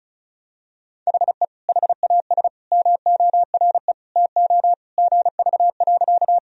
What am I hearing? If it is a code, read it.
Morse code, heard as a keyed tone, about 35 wpm, 700 Hz. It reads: HE HAS MORE TO GV.
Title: Sentences from Top 200 Words and Abbreviations 35wpm